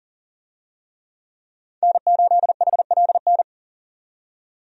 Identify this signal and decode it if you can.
Morse code, heard as a keyed tone, about 40 wpm, 700 Hz. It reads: N8HLN
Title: USA Call Signs - 5 Characters - 40wpm